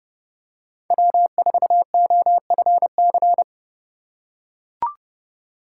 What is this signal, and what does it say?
Morse code, heard as a keyed tone, about 30 wpm, 700 Hz. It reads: W4OFC
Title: USA Call Signs - 2x Character Spacing - 30wpm